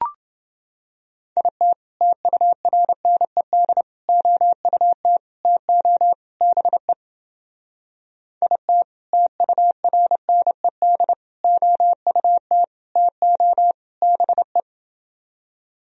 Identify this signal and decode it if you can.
Morse code, heard as a keyed tone, about 29 wpm, 700 Hz. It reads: IT TURNED OUT TO BE IT TURNED OUT TO BE